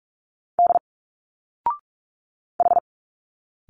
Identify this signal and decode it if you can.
Morse code, heard as a keyed tone, about 45 wpm, 700 Hz. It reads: D H
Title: Single Letters 45wpm